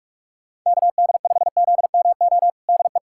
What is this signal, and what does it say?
Morse code, heard as a keyed tone, about 42 wpm, 700 Hz. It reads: KD5ZMO DE
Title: USA Callsign DE Callsign - 6 Characters - 45wpm